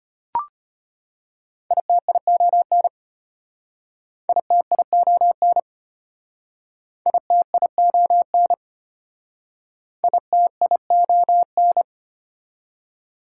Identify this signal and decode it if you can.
Morse code, heard as a keyed tone, about 32 wpm, 700 Hz. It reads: ITION ITION ITION ITION